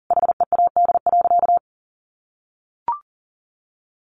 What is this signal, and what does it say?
Morse code, heard as a keyed tone, about 40 wpm, 700 Hz. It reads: HEAD.
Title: Sentences from Top 1100 Words 40wpm